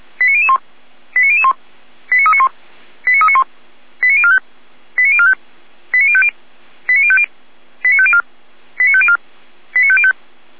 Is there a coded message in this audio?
Yes: five-tone selective call (ZVEI) from 0.2 to 10.1 s, heard as quick runs of five tones, tones of about 70 ms. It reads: 89001 89001 89391 89391 89045 89045 89050 89050 89504 89504 89505